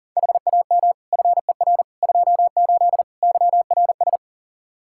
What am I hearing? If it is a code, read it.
Morse code, heard as a keyed tone, about 37 wpm, 700 Hz. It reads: HAM FER 28 YRS
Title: Practice QSO - Exchange 3 - More Info - 40wpm